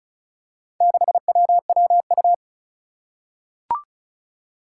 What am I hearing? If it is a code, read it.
Morse code, heard as a keyed tone, about 34 wpm, 700 Hz. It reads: 6WWU
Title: Four Challenging Letter-Number Combinations 35wpm